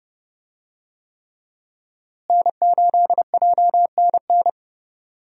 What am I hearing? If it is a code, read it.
Morse code, heard as a keyed tone, about 30 wpm, 700 Hz. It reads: N8JNN
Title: USA Call Signs - 5 Characters - Repeated 3x - 30wpm